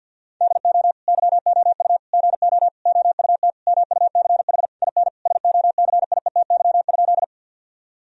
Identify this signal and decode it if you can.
Morse code, heard as a keyed tone, about 50 wpm, 700 Hz. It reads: DO YOU GO OUT MUCH IN SOCIETY?
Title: Sentences from Top 1000 Words - Review Part 4 - 50wpm